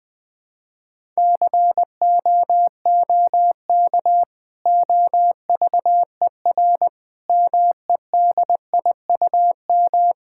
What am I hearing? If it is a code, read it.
Morse code, heard as a keyed tone, about 21 wpm, 700 Hz. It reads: COOK OVER MEDIUM
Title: Sentences from Top 1000 Words - Review Part 6 - 20wpm